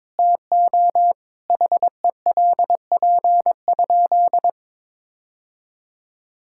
Morse code, heard as a keyed tone, about 22 wpm, 700 Hz. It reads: TO HELP?
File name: practice / 22wpm / Sentences with 4-Word Phrases - 22wpm